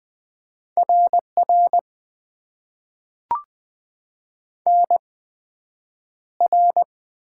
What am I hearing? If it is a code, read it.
Morse code, heard as a keyed tone, about 20 wpm, spaced as for 9 wpm, 700 Hz. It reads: RR NR
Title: Top 90 QSO Elements - 8x Character Spacing - 20wpm